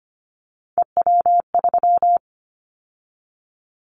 Morse code, heard as a keyed tone, about 25 wpm, 700 Hz. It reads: EW3
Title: Callsign Prefixes for Top 100 DXCC - 25wpm